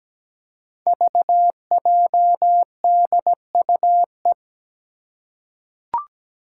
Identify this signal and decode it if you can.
Morse code, heard as a keyed tone, about 17 wpm, 700 Hz. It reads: VJDUE
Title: Five Challenging Letter-Number Combinations 17wpm